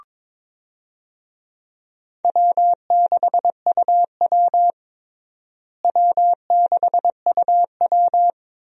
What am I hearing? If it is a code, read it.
Morse code, heard as a keyed tone, about 22 wpm, 700 Hz. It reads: W6UW W6UW
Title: USA Call Signs - 4 Characters - Repeated 2x - 22wpm